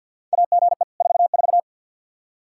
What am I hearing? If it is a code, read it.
Morse code, heard as a keyed tone, about 53 wpm, 700 Hz. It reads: AGE 44
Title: Practice QSO - Exchange 3 - More Info - 50wpm